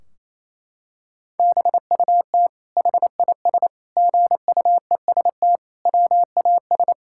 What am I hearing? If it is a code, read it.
Morse code, heard as a keyed tone, about 28 wpm, 700 Hz. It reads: BUT HIS GUEST WAS
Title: Sentences from Top 1600 Words 28wpm